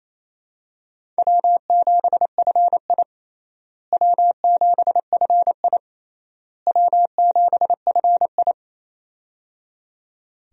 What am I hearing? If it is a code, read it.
Morse code, heard as a keyed tone, about 28 wpm, 700 Hz. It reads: W7FI W7FI W7FI
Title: USA Call Signs - 4 Characters - Repeated 3x - 28wpm